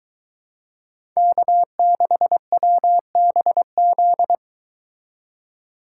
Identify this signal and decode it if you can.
Morse code, heard as a keyed tone, about 23 wpm, 700 Hz. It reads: K6WBZ